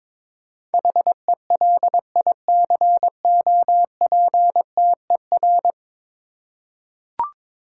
Morse code, heard as a keyed tone, about 22 wpm, 700 Hz. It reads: HELICOPTER
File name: practice / 22wpm / 10 Letter Words - First 3 Letters Given - 22wpm